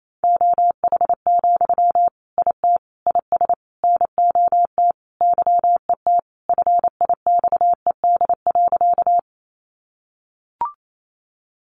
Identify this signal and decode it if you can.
Morse code, heard as a keyed tone, about 28 wpm, 700 Hz. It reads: OH, IT IS NOT YET FIXED.
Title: Sentences from Top 1100 Words 28wpm